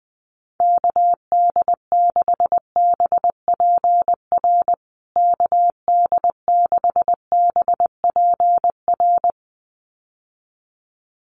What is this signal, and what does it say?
Morse code, heard as a keyed tone, about 20 wpm, 700 Hz. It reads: KD6BPR KD6BPR
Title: USA Call Signs Repeated 2x 20wpm